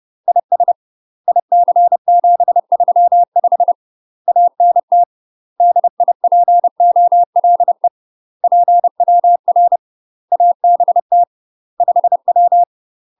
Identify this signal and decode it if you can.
Morse code, heard as a keyed tone, about 30 wpm, 700 Hz. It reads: IS IC735 ANT DIPOLE PWR ABT 5W
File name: UK MASTER QSOs TRAINING / 30WPM / UK MASTER QSOs 0044